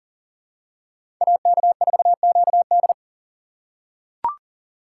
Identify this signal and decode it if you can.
Morse code, heard as a keyed tone, about 40 wpm, 700 Hz. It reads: AK4QD
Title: USA Call Signs - 5 Characters - 6x Character Spacing - 40wpm